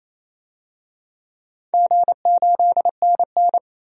Morse code, heard as a keyed tone, about 27 wpm, 700 Hz. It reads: G8NN
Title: DXCC-100 Call Signs - 4 Characters - 28wpm